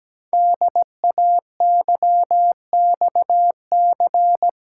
Morse code, heard as a keyed tone, about 17 wpm, 700 Hz. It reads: DAYXC
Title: Code Groups - 5 Letters - 17wpm